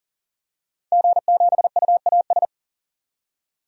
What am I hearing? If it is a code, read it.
Morse code, heard as a keyed tone, about 40 wpm, 700 Hz. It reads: G7UAS
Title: DXCC-100 Call Signs - 5 Characters - 40wpm